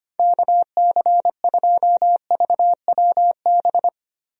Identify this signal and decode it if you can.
Morse code, heard as a keyed tone, about 25 wpm, 700 Hz. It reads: KC2VWB
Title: USA Call Signs - 6 Characters - Repeated 3x - 25wpm